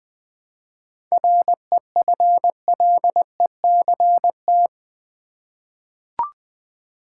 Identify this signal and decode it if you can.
Morse code, heard as a keyed tone, about 20 wpm, 700 Hz. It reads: REFLECT
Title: 7 Letter Words - First 3 Letters Given - 20wpm